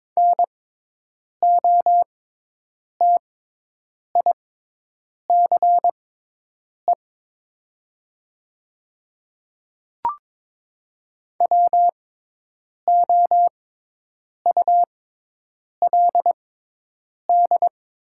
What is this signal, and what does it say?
Morse code, heard as a keyed tone, about 22 wpm, spaced as for 8 wpm, 700 Hz. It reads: NOTICE WOULD
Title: 6x Character Spacing - Top 500 Words - 22wpm